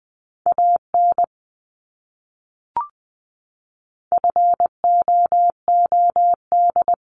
Morse code, heard as a keyed tone, about 20 wpm, 700 Hz. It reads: AN FOOD
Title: Top 300 Words Repeated 20wpm